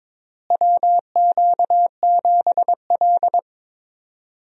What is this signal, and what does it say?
Morse code, heard as a keyed tone, about 22 wpm, 700 Hz. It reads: WQ7L
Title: USA Call Signs - 4 Characters - Repeated 2x - 22wpm